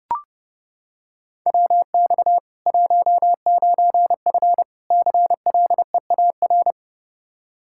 Morse code, heard as a keyed tone, about 30 wpm, 700 Hz. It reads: WX 19F CLEAR
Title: Practice QSO - Exchange 2 - Info - 30wpm